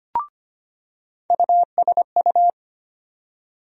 Morse code, heard as a keyed tone, about 25 wpm, 700 Hz. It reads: USU
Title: USA Call Sign Postfixes - 3 Letters 25wpm